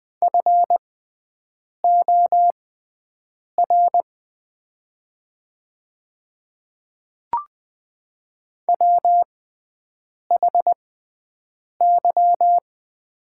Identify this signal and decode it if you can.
Morse code, heard as a keyed tone, about 20 wpm, spaced as for 7 wpm, 700 Hz. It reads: FOR WHY
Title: 6x Character Spacing - Top 200 Words - 20wpm